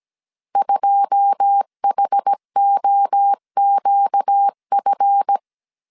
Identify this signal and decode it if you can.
Morse code, heard as a keyed tone, about 17 wpm, 800 Hz. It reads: 2HOQF
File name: CW / DET / 1 9 1500C S A 16wpm 2828